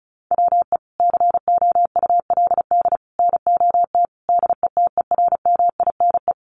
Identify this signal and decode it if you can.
Morse code, heard as a keyed tone, about 34 wpm, 700 Hz. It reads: WE COULD NOT DETERMINE
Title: Sentences from Top 1300 Words 35wpm